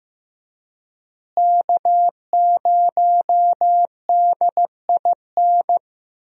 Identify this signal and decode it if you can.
Morse code, heard as a keyed tone, about 15 wpm, 700 Hz. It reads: K0DIN
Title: USA Call Signs - 5 Characters - Repeated 3x - 15wpm